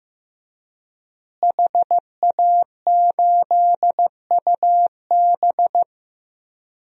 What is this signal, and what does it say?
Morse code, heard as a keyed tone, about 15 wpm, 700 Hz. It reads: HA8UB
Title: DXCC-100 Call Signs - 5 Characters - Repeated 3x - 15wpm